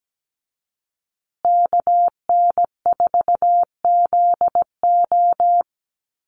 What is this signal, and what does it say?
Morse code, heard as a keyed tone, about 17 wpm, 700 Hz. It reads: KN4ZO
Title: USA Call Signs Repeated 2x 17wpm